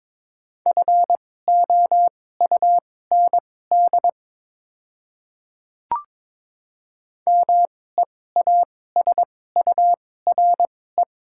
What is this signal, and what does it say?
Morse code, heard as a keyed tone, about 22 wpm, spaced as for 16 wpm, 700 Hz. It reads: FOUND MEASURE
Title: 2x Character Spacing - Top 500 Words - 22wpm